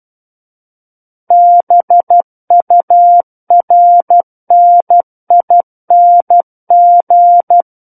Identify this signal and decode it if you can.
Morse code, heard as a keyed tone, about 12 wpm, 700 Hz. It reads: BURNING
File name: Book One / 03-Chapter 3 - 12 WPM